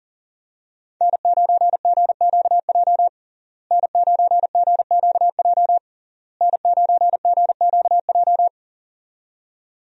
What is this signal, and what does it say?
Morse code, heard as a keyed tone, about 40 wpm, 700 Hz. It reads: N9GQJ N9GQJ N9GQJ
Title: USA Call Signs - 5 Characters - Repeated 3x - 40wpm